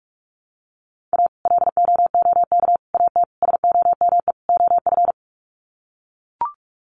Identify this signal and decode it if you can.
Morse code, heard as a keyed tone, about 44 wpm, 700 Hz. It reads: A LOOK AT SOME OF
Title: Sets of 5 Words Encore2 45wpm